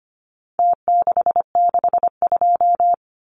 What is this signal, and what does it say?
Morse code, heard as a keyed tone, about 25 wpm, 700 Hz. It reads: T662
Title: Numbers - Contest Style - 4-Digit 25wpm